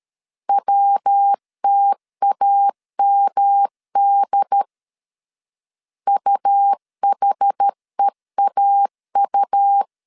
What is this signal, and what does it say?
Morse code, heard as a keyed tone, about 13 wpm, 800 Hz. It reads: WTAMD UHEAU
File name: CW / DET / 1 3 1500C S C 12wpm 2750